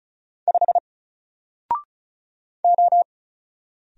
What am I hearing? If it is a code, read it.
Morse code, heard as a keyed tone, about 36 wpm, 700 Hz. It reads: 5 O
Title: Single Character (Letter:Number:Punctuation:Prosign) Alt 35wpm